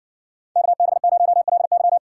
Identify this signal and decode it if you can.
Morse code, heard as a keyed tone, about 59 wpm, 700 Hz. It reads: KB0LY